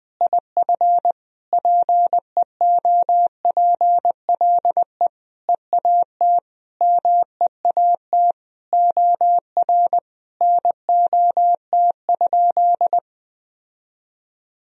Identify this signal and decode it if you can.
Morse code, heard as a keyed tone, about 20 wpm, 700 Hz. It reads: IF PEOPLE EAT MEAT OR NOT?
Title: Sentences from Top 1000 Words - Review Part 3 - 20wpm